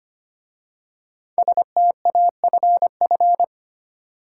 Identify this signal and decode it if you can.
Morse code, heard as a keyed tone, about 25 wpm, 700 Hz. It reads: STAFF
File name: 5 Letter Words - First 3 Letters Given - 25wpm